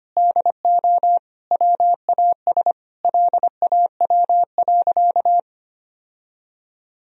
Morse code, heard as a keyed tone, about 25 wpm, 700 Hz. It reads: DO WAS LAW.
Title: Sentences from Top 1000 Words - Review Part 3 - 25wpm